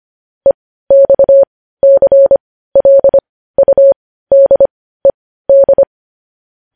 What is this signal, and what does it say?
Morse code, heard as a keyed tone, about 25 wpm, spaced as for 15 wpm, 550 Hz. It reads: EXCLUDED
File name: podcast / News headlines-1x-25x15